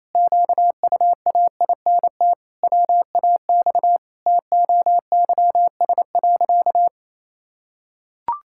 Morse code, heard as a keyed tone, about 28 wpm, 700 Hz. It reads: QUAINT WAX TOYS.
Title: Pangram Challenge v2 - 28wpm